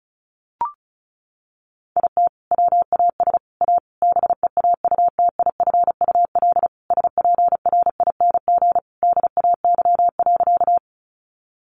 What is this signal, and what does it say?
Morse code, heard as a keyed tone, about 34 wpm, 700 Hz. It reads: IT WAS A BEAUTIFUL SPRING DAY.
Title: Sentences from Top 1200 Words 35wpm